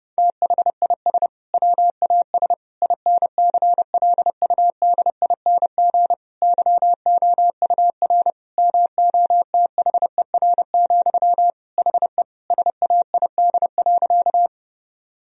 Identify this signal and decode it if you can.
Morse code, heard as a keyed tone, about 32 wpm, 700 Hz. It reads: THIS WAS INCLUDING YOUR MOTHER, HE SAID.
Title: Sentences from Top 1000 Words - Review Part 4 - 30wpm